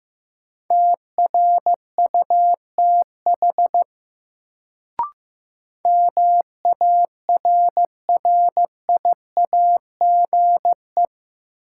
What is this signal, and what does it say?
Morse code, heard as a keyed tone, about 15 wpm, 700 Hz. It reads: TRUTH MARRIAGE
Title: Sentences from Top 800 Words - New Word Drill - 15wpm